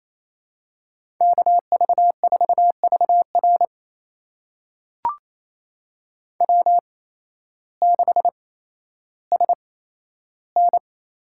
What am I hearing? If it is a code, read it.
Morse code, heard as a keyed tone, about 28 wpm, spaced as for 13 wpm, 700 Hz. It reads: KV4VR W6SN
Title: USA Call Signs - 8x Character Spacing - 28wpm